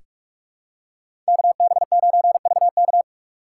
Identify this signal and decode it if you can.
Morse code, heard as a keyed tone, about 43 wpm, 700 Hz. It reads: KB9VK
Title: USA Call Signs - 5 Characters - 2x Character Spacing - 45wpm